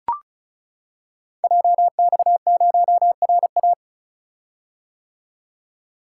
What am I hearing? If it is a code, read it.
Morse code, heard as a keyed tone, about 36 wpm, 690 Hz. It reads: JX0RA
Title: Code Groups - 5 Letter:Numbers - 35wpm